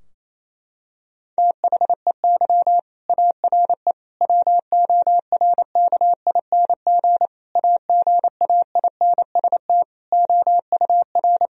Morse code, heard as a keyed tone, about 27 wpm, 700 Hz. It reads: THEY ARE WORKING AGAINST OUR
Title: Sentences from Top 1000 Words - Review Part 4 - 28wpm